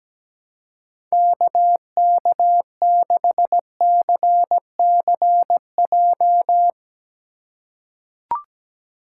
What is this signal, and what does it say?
Morse code, heard as a keyed tone, about 17 wpm, 700 Hz. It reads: KK6CCJ
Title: USA Call Signs - 6 Characters - 2x Character Spacing - 17wpm